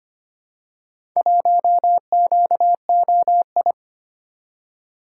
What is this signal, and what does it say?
Morse code, heard as a keyed tone, about 25 wpm, 700 Hz. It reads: 1QOI
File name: practice / 25wpm / Four Letter-Number Combinations - 25wpm